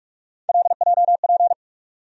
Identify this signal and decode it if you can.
Morse code, heard as a keyed tone, about 42 wpm, 700 Hz. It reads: LJP
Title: USA Call Sign Postfixes - 3 Letters 45wpm